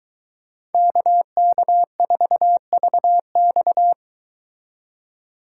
Morse code, heard as a keyed tone, about 23 wpm, 700 Hz. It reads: KK4VX